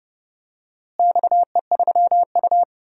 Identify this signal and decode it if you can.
Morse code, heard as a keyed tone, about 30 wpm, 700 Hz. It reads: XE3U
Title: DXCC-100 Call Signs - 4 Characters - Repeated 2x - 30wpm